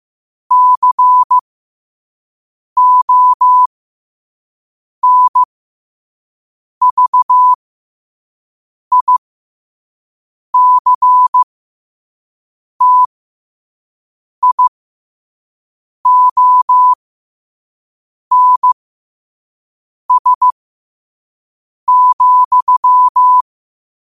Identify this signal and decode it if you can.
Morse code, heard as a keyed tone, about 15 wpm, spaced as for 5 wpm, 1000 Hz. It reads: CONVICTIONS,
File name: qotdfiles / 5wpm